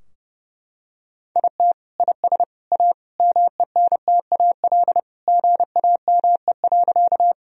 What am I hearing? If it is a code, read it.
Morse code, heard as a keyed tone, about 32 wpm, 700 Hz. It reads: IT IS A MENTAL GAME.
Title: Sentences from Top 1000 Words - Review Part 4 - 30wpm